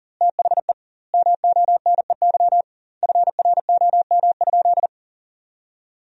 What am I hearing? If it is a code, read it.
Morse code, heard as a keyed tone, about 37 wpm, 700 Hz. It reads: THE MONEY FROM?
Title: 2x Word-Spacing - Sentences from Top 400 Words - 40wpm